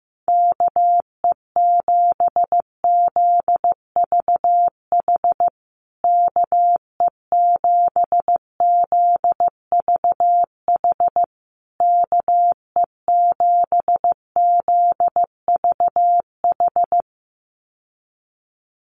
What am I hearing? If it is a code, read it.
Morse code, heard as a keyed tone, about 15 wpm, 700 Hz. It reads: KE7ZVH KE7ZVH KE7ZVH
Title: USA Call Signs Repeated 3x 15wpm